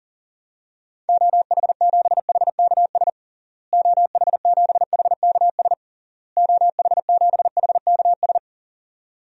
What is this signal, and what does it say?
Morse code, heard as a keyed tone, about 37 wpm, 700 Hz. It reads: OH7HKS OH7HKS OH7HKS
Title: DXCC-100 Call Signs - 6 Characters - Repeated 3x - 40wpm